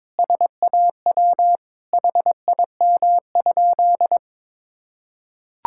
Morse code, heard as a keyed tone, about 22 wpm, 700 Hz. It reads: SAW HIM?
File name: Sentences with 7-Word Phrases - 22wpm